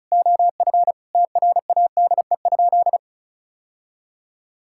Morse code, heard as a keyed tone, about 36 wpm, 700 Hz. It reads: OF TRADE?
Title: Sentences from Top 1000 Words - Review Part 6 - 35wpm